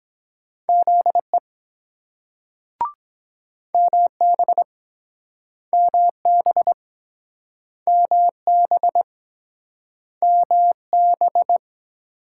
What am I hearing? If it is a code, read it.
Morse code, heard as a keyed tone, about 22 wpm, 700 Hz. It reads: ZE MB MB MB MB